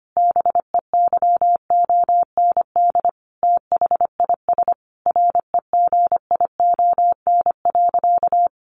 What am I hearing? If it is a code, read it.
Morse code, heard as a keyed tone, about 25 wpm, 700 Hz. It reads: BEYOND THIS REGION.
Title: Sentences from Top 1500 Words 25wpm